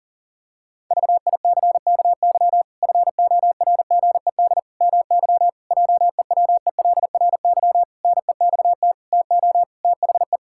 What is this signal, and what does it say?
Morse code, heard as a keyed tone, about 40 wpm, 700 Hz. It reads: VICKY FORGED MY JEWELRY NEXT TO THE
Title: Pangram Challenge v2 - 40wpm